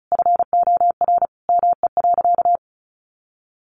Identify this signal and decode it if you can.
Morse code, heard as a keyed tone, about 34 wpm, 700 Hz. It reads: FOR ME.
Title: Sentences from Top 1500 Words 35wpm